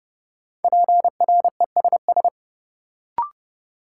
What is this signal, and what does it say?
Morse code, heard as a keyed tone, about 30 wpm, 700 Hz. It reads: PRESS
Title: Sentences from Top 800 Words - New Word Drill - 30wpm